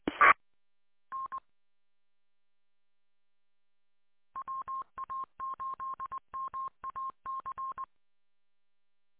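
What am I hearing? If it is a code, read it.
Morse code, heard as a keyed tone, about 20 wpm, 1070 Hz. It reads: T N WA8MAC
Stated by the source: Morse from the WA8MAC 147.200 repeater audio archive.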